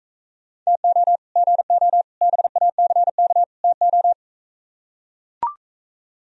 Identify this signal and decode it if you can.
Morse code, heard as a keyed tone, about 44 wpm, 700 Hz. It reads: TO GO BACK TO